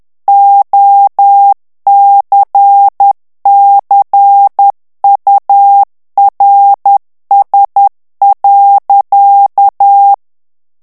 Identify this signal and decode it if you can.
Morse code, heard as a keyed tone, about 10 wpm, 800 Hz. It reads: OCCURS.
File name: ANGLAIS texte clair 3 vitesse 10 mots minute